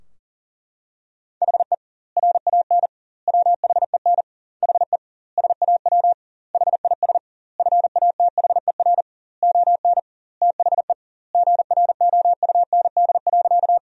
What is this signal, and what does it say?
Morse code, heard as a keyed tone, about 37 wpm, 700 Hz. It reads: HE RAN WHEN HE SAW HIS FATHER ON THE GROUND.
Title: 2x Word-Spacing - Sentences from Top 500 Words - 40wpm